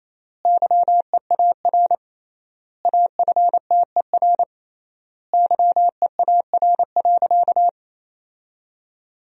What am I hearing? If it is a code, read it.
Morse code, heard as a keyed tone, about 28 wpm, 700 Hz. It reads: YEAR AFTER YEAR.
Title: 3x Word-Spacing - Sentences from Top 300 Words - 28wpm